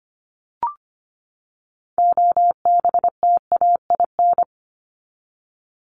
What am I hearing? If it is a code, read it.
Morse code, heard as a keyed tone, about 25 wpm, 700 Hz. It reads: OBTAIN
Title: ED Suffix - Top 100 Words with Base Words 25wpm